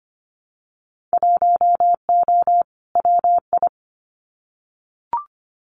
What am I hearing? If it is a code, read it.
Morse code, heard as a keyed tone, about 25 wpm, 700 Hz. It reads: 1O WI
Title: Winter Field Day Exchanges 25wpm